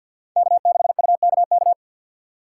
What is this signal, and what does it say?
Morse code, heard as a keyed tone, about 53 wpm, 700 Hz. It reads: K6UKK
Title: USA Call Signs - 6x Character Spacing - 50wpm